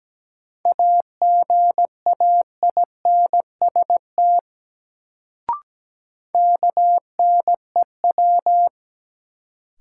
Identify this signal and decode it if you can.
Morse code, heard as a keyed tone, about 17 wpm, 700 Hz. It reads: AGAINST KNEW
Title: Sentences from Top 400 Words - New Word Drill - 17wpm